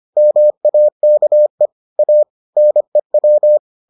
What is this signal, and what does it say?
Morse code, heard as a keyed tone, about 25 wpm, 600 Hz. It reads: MAKE A NEW